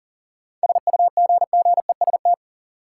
Morse code, heard as a keyed tone, about 38 wpm, 700 Hz. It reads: SUGGEST
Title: Sentences from Top 1300 Words - New Word Drill - 40wpm